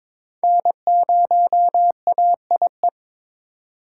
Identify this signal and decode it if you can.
Morse code, heard as a keyed tone, about 21 wpm, 700 Hz. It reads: N0AIE
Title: K1USN SST QSO Practice - 22wpm